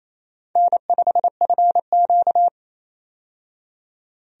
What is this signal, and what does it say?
Morse code, heard as a keyed tone, about 28 wpm, 700 Hz. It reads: N5FQ
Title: USA Call Signs - 4 Characters - 28wpm